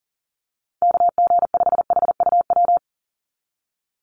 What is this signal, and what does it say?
Morse code, heard as a keyed tone, about 37 wpm, 700 Hz. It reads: KG5HUW
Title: USA Call Signs 40wpm